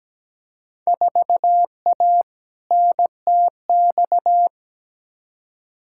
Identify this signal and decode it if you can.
Morse code, heard as a keyed tone, about 17 wpm, 700 Hz. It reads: 4A NTX
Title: Field Day Exchanges - 17wpm